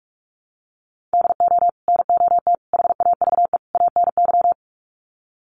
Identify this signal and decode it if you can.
Morse code, heard as a keyed tone, about 44 wpm, 700 Hz. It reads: DO NOT HAVE ANY
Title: Sets of 4 Words 45wpm